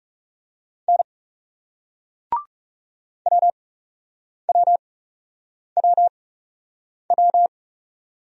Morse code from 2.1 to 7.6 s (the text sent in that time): WWWW